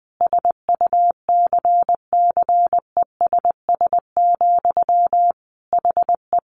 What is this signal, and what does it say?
Morse code, heard as a keyed tone, about 20 wpm, 700 Hz. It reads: SUCCESS, HE
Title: Sentences from Top 900 Words 20wpm